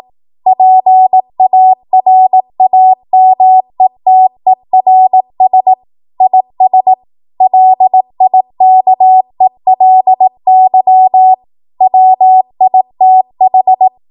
Morse code, heard as a keyed tone, about 18 wpm, 750 Hz. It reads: PARAMETERS IS LIKELY WITH
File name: Morse / 18WPM